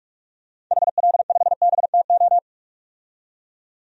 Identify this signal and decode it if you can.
Morse code, heard as a keyed tone, about 41 wpm, 700 Hz. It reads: HL5BTO
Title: DXCC-100 Call Signs - 6 Characters - 45wpm